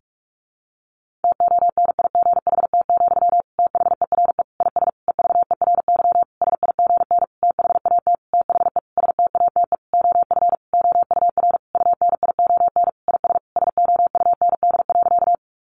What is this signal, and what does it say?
Morse code, heard as a keyed tone, about 44 wpm, 700 Hz. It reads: TONIGHT, THERE IS EVERY SIGN THAT THE STATE OF OUR UNION IS SOUND.
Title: Sentences from Top 1400 Words 45wpm